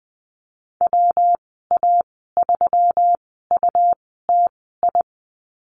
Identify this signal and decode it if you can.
Morse code, heard as a keyed tone, about 20 wpm, spaced as for 14 wpm, 700 Hz. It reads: WA3UTI
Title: 179 Lesson 29g - Slash - Call Signs 20wpm